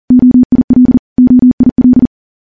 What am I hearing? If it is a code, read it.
Morse code, heard as a keyed tone, about 40 wpm, 250 Hz. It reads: OIL OIL
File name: morse-training-40WPM